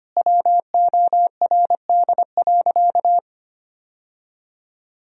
Morse code, heard as a keyed tone, about 25 wpm, 700 Hz. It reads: WORD.
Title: Sentences from Top 1000 Words - Review Part 4 - 25wpm